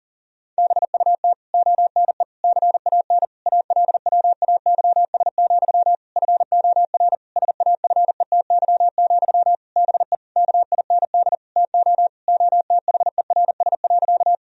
Morse code, heard as a keyed tone, about 40 wpm, 700 Hz. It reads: BUT ONE CAN ALWAYS, FOR SAFETY, BE KIND TO OTHERS.
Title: Sentences from Top 1000 Words - Review Part 4 - 40wpm